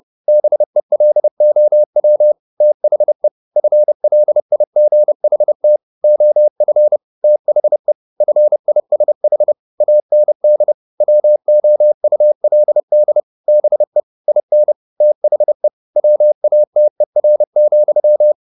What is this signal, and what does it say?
Morse code, heard as a keyed tone, about 30 wpm, 600 Hz. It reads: BELOW THE FLIGHT OF THE FISH AND WOULD BE IN THE WATER,
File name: the old man and the sea / the old man and the sea 1 30wpm